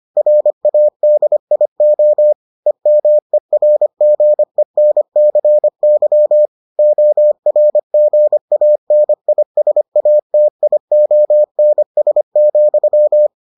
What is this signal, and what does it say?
Morse code, heard as a keyed tone, about 25 wpm, 600 Hz. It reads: RADIO EMERGENCY ORGANISATIONS,